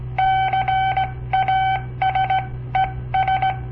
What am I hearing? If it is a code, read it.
Morse code, heard as a keyed tone, about 18 wpm, spaced as for 15 wpm, 770 Hz. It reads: CASES
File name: papertapes / local / Code Practice-01